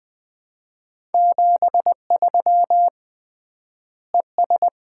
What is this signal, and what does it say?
Morse code, heard as a keyed tone, about 20 wpm, 700 Hz. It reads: 73 ES
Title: Common QSO Phrases - 3x Word-Spacing - 20wpm